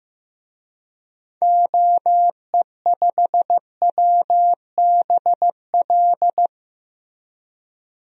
Morse code, heard as a keyed tone, about 15 wpm, 700 Hz. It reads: OE5WBL
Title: DXCC-100 Call Signs - 6 Characters - Repeated 2x - 15wpm